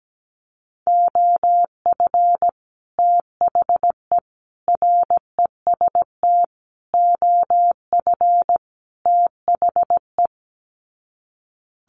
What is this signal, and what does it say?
Morse code, heard as a keyed tone, about 17 wpm, 700 Hz. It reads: OF THE REST OF THE
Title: Sets of 5 Words 17wpm